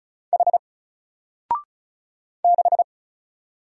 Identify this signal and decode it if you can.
Morse code, heard as a keyed tone, about 33 wpm, 700 Hz. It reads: H 6
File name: Single-Letter-Number-35wpm